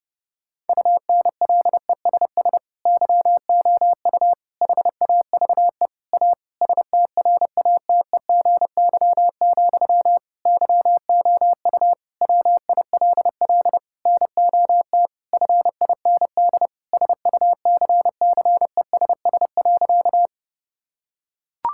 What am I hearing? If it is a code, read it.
Morse code, heard as a keyed tone, about 30 wpm, 700 Hz. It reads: UNLESS YOU HAVE A STRATEGY, YOU WILL NOT FIND SUCCESS.
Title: Sentences from Top 1000 Words - Review Part 5 - 30wpm